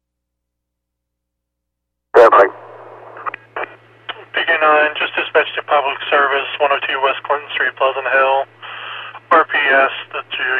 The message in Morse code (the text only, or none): none